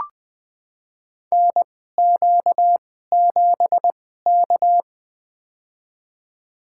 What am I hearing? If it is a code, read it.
Morse code, heard as a keyed tone, about 20 wpm, spaced as for 15 wpm, 700 Hz. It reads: NQ7K
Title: USA Call Signs - 4 Characters - 2x Character Spacing - 20wpm